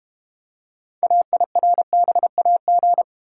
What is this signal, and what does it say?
Morse code, heard as a keyed tone, about 32 wpm, 700 Hz. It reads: AIRBAG